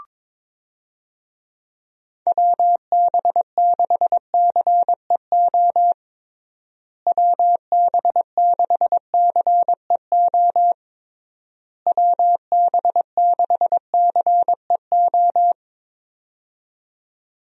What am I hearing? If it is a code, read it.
Morse code, heard as a keyed tone, about 22 wpm, 700 Hz. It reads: WB6CEO WB6CEO WB6CEO
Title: USA Call Signs - 6 Characters - Repeated 3x - 22wpm